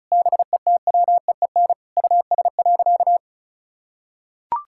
Morse code, heard as a keyed tone, about 36 wpm, 700 Hz. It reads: BETWEEN US.
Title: Sentences from Top 1000 Words - Review Part 6 - 35wpm